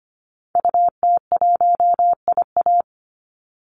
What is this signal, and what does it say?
Morse code, heard as a keyed tone, about 25 wpm, 700 Hz. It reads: UT1IA
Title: Call Signs Repeated 3x - 25wpm